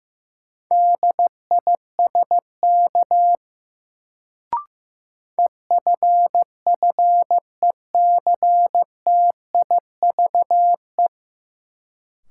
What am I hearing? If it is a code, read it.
Morse code, heard as a keyed tone, about 15 wpm, 700 Hz. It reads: DISK EFFECTIVE
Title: Sentences from Top 1000 Words - Review Part 6 - New Word Drill - 15wpm